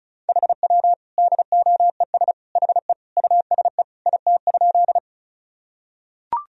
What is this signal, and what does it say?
Morse code, heard as a keyed tone, about 34 wpm, 700 Hz. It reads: HW DOES HE USE IT?
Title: Sentences from Top 200 Words and Abbreviations 35wpm